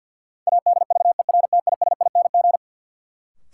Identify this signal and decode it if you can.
Morse code, heard as a keyed tone, about 47 wpm, 700 Hz. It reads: ADVERTISING
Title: Sentences from Top 1000 Words - Review Part 6 - New Word Drill - 50wpm